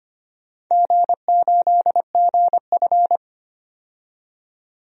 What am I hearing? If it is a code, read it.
Morse code, heard as a keyed tone, about 25 wpm, 700 Hz. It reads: G8GF
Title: DXCC-100 Call Signs - 4 Characters - 25wpm